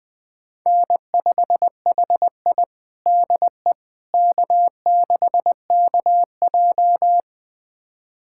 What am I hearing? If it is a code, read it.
Morse code, heard as a keyed tone, about 20 wpm, 700 Hz. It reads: N5HI DE K6KJ
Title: USA Callsign DE Callsign - 4 Characters - 20wpm